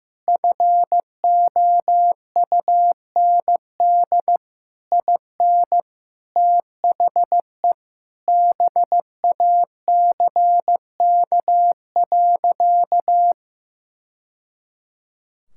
Morse code, heard as a keyed tone, about 15 wpm, 700 Hz. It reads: FOUND IN THE BACK.
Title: Sentences from Top 1000 Words - Review Part 5 - 15wpm